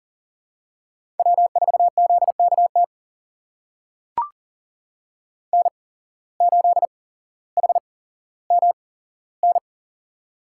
Morse code, heard as a keyed tone, about 39 wpm, spaced as for 17 wpm, 700 Hz. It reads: W4ZKT N8HMN
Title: USA Call Signs - 8x Character Spacing - 40wpm